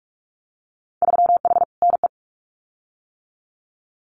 Morse code, heard as a keyed tone, about 46 wpm, 700 Hz. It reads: 3H NE
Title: Winter Field Day Exchanges Encore 45wpm